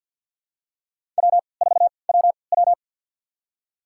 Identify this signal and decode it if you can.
Morse code, heard as a keyed tone, about 52 wpm, spaced as for 29 wpm, 700 Hz. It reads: W4WW
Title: USA Call Signs - 4 Characters - 3x Character Spacing - 50wpm